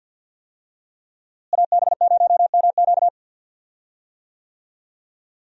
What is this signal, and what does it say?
Morse code, heard as a keyed tone, about 52 wpm, 700 Hz. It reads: AB0MQ
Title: USA Call Signs - 5 Characters - Repeated 2x - 50wpm